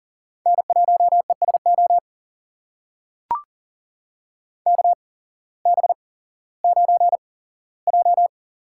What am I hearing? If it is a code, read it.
Morse code, heard as a keyed tone, about 38 wpm, spaced as for 18 wpm, 700 Hz. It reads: N1ESO KB9J
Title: USA Call Signs - 8x Character Spacing - 40wpm